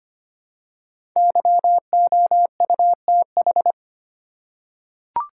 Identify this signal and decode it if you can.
Morse code, heard as a keyed tone, about 25 wpm, 700 Hz. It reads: YOUTH
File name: 5 Letter Words - First 3 Letters Given - 25wpm